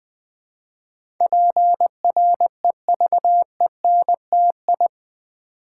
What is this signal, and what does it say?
Morse code, heard as a keyed tone, about 20 wpm, 700 Hz. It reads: PREVENTI
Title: Build A Word - 4 Letters At A Time - 10-Letter Words - 20wpm